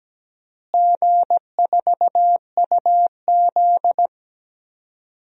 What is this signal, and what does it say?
Morse code, heard as a keyed tone, about 17 wpm, 700 Hz. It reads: G4UZ
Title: DXCC-100 Call Signs - 4 Characters - Repeated 2x - 17wpm